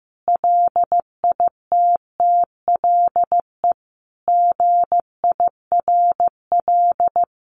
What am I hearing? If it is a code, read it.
Morse code, heard as a keyed tone, about 15 wpm, 700 Hz. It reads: LITTLE GIRL
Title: Sentences from Top 1100 Words 15wpm